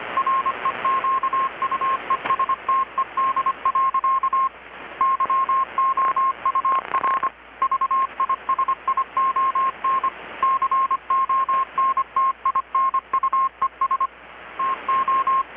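Morse code, heard as a keyed tone, about 25 wpm, 1070 Hz, with radio noise. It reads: REQUESTED. YOUR VISION CONTINUES TO
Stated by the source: Morse from KPH's Night of Nights XX maritime radiotelegraphy event, received in Chiapas, Mexico about 3,800 km away on a Kenwood R-600 with a grounded 10 m long-wire antenna.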